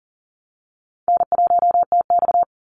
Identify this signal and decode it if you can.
Morse code, heard as a keyed tone, about 39 wpm, 700 Hz. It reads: N1TX
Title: Easy Call Signs Repeated 2x - 40wpm